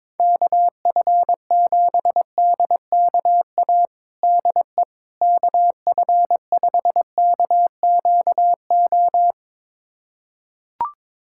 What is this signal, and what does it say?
Morse code, heard as a keyed tone, about 22 wpm, 700 Hz. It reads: KF7DKA DE KF5KQO
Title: USA Callsign DE Callsign - 6 Characters - 22wpm